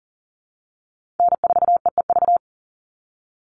Morse code, heard as a keyed tone, about 41 wpm, 700 Hz. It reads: N4EEV
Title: Call Signs - 40wpm